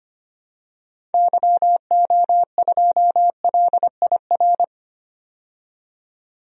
Morse code, heard as a keyed tone, about 25 wpm, 700 Hz. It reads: YO2LIR
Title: DXCC-100 Call Signs - 6 Characters - 25wpm